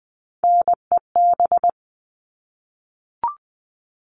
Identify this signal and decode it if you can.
Morse code, heard as a keyed tone, about 20 wpm, 700 Hz. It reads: NEB
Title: All Cut Numbers - 3-Digit 20wpm